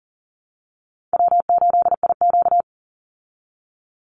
Morse code, heard as a keyed tone, about 40 wpm, 700 Hz. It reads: W8IQ
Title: Easy Call Signs Repeated 3x - 40wpm